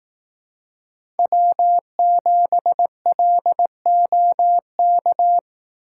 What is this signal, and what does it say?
Morse code, heard as a keyed tone, about 18 wpm, 700 Hz. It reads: W7LOK
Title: Speed Racing - USA 1x3 Call Signs 15wpm